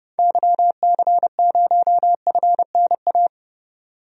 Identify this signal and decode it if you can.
Morse code, heard as a keyed tone, about 30 wpm, 700 Hz. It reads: YC0FNA
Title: DXCC-100 Call Signs - 6 Characters - 30wpm